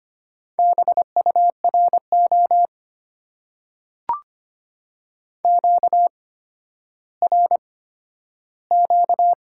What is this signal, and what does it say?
Morse code, heard as a keyed tone, about 25 wpm, spaced as for 12 wpm, 700 Hz. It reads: BURO QRQ
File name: Top 90 QSO Elements - 8x Character Spacing - 25wpm